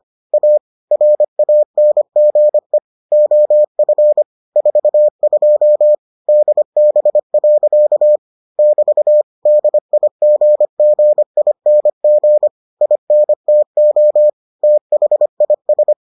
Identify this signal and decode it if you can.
Morse code, heard as a keyed tone, about 25 wpm, 600 Hz. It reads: A RANGE OF 42 DB. <BT> DIGGING INTO THIS